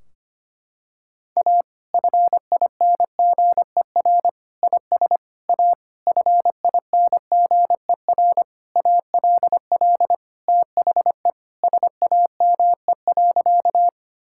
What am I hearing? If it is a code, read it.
Morse code, heard as a keyed tone, about 25 wpm, 700 Hz. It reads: A FINGER IS A FINGER ALL THE SAME.
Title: Sentences from Top 1600 Words 25wpm